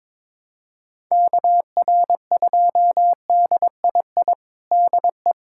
Morse code, heard as a keyed tone, about 21 wpm, 700 Hz. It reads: KR2DII DE
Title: USA Callsign DE Callsign 22wpm